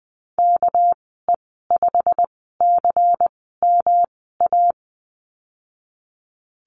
Morse code, heard as a keyed tone, about 20 wpm, 700 Hz. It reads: K E 5 C M A
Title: 238 Lesson 38g - Letter q - Call Signs 20wpm